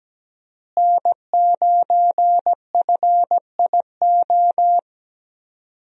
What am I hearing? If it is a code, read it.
Morse code, heard as a keyed tone, about 17 wpm, 700 Hz. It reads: N9FIO
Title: USA Call Signs - 5 Characters - Repeated 3x - 17wpm